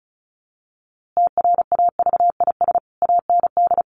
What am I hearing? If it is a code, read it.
Morse code, heard as a keyed tone, about 34 wpm, 700 Hz. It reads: TRAVIS AND
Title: A Sound Of Thunder 35wpm